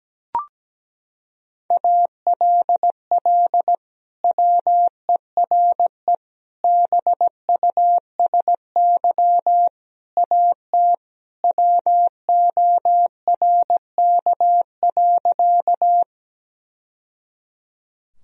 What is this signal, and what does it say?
Morse code, heard as a keyed tone, about 17 wpm, 700 Hz. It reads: ALL WERE BUSY AT WORK.
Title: Sentences from Top 1000 Words - Review Part 3 - 17wpm